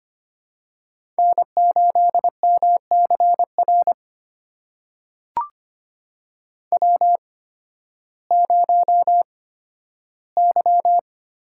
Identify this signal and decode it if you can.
Morse code, heard as a keyed tone, about 25 wpm, spaced as for 13 wpm, 700 Hz. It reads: N8MCR W0Y
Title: USA Call Signs - 5 Characters - 8x Character Spacing - 25wpm